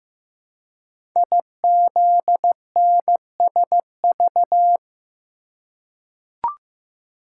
Morse code, heard as a keyed tone, about 15 wpm, 700 Hz. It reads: IZNSV
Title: Code Groups - 5 Letters - 15wpm